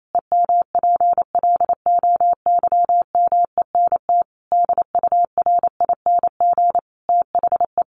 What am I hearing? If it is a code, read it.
Morse code, heard as a keyed tone, about 28 wpm, 700 Hz. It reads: EMPLOYMENT DURING THE
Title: Sentences from Top 1500 Words 28wpm